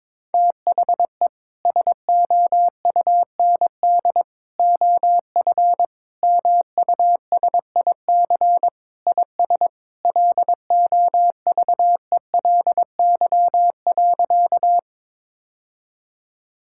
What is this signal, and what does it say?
Morse code, heard as a keyed tone, about 22 wpm, 700 Hz. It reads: THE SOUND OF MUSIC IS LOVELY.
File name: practice / 22wpm / Sentences with 3-Word Phrases - 22wpm